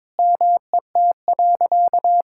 Morse code, heard as a keyed tone, about 22 wpm, 700 Hz. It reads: MET.
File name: practice / 22wpm / Sentences with 3-Word Phrases - 22wpm